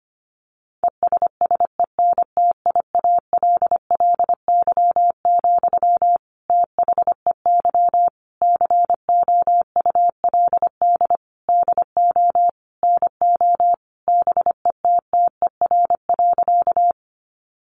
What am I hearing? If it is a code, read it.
Morse code, heard as a keyed tone, about 25 wpm, 700 Hz. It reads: ESSENTIALLY, THEY COULD DO NO BETTER.
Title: Sentences from Top 1300 Words 25wpm